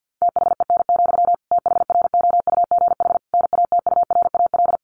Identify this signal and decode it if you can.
Morse code, heard as a keyed tone, about 50 wpm, 700 Hz. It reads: THEN, THROUGH NATURAL
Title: Sentences from Top 1400 Words 50wpm